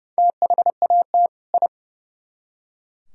Morse code, heard as a keyed tone, about 37 wpm, 700 Hz. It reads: THAT I
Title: Speed-Racing - Sets of 2 Words 30wpm